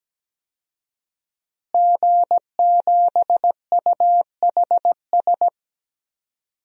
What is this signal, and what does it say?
Morse code, heard as a keyed tone, about 17 wpm, 700 Hz. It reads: G7UHS
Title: DXCC-100 Call Signs - 5 Characters - Repeated 3x - 17wpm